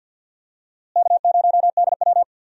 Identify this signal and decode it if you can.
Morse code, heard as a keyed tone, about 50 wpm, 700 Hz. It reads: K0DW
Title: USA Call Signs - 4 Characters - Repeated 3x - 50wpm